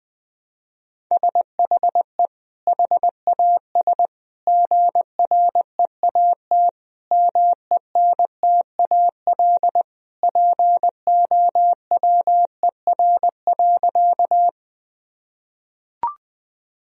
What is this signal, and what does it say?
Morse code, heard as a keyed tone, about 21 wpm, 700 Hz. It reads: SHE HAS GREAT MENTAL POWER.
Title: Sentences from Top 1000 Words - Review Part 4 - 20wpm